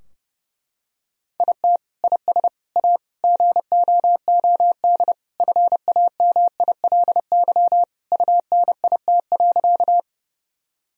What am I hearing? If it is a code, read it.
Morse code, heard as a keyed tone, about 29 wpm, 700 Hz. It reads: IT IS A GOOD FAMILY UNIT.
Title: Sentences from Top 1000 Words - Review Part 2 - 30wpm